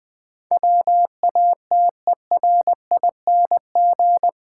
Morse code, heard as a keyed tone, about 21 wpm, 700 Hz. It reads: WATERING
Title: Difficult Sentences with Hints - 20wpm